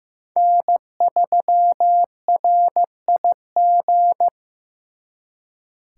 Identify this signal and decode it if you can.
Morse code, heard as a keyed tone, about 18 wpm, 700 Hz. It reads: N3RIG
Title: Speed Racing - USA 1x3 Call Signs 15wpm